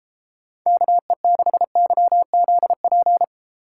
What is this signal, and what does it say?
Morse code, heard as a keyed tone, about 32 wpm, 700 Hz. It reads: KE6YZP